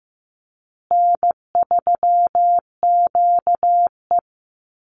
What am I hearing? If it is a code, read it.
Morse code, heard as a keyed tone, about 15 wpm, 700 Hz. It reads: N3QE
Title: Easy Call Signs Repeated 3x - 15wpm